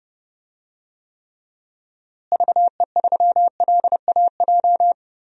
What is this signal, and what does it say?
Morse code, heard as a keyed tone, about 32 wpm, 700 Hz. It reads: VE3LAJ
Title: DXCC-100 Call Signs - 6 Characters - Repeated 2x - 30wpm